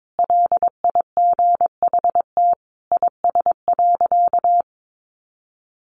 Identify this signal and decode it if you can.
Morse code, heard as a keyed tone, about 22 wpm, 700 Hz. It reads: LIGHT IS.
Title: Sentences from Top 1000 Words 22wpm